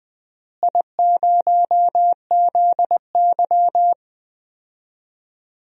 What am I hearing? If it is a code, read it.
Morse code, heard as a keyed tone, about 20 wpm, 700 Hz. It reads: I0ZY
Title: DXCC-100 Call Signs - 4 Characters - Repeated 2x - 20wpm